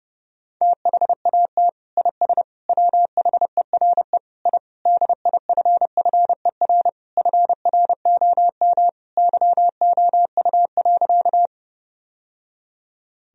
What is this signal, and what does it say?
Morse code, heard as a keyed tone, about 29 wpm, 700 Hz. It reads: THAT IS WHERE I DIFFER FROM YOU.
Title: Sentences from Top 1000 Words - Review Part 1 - 30wpm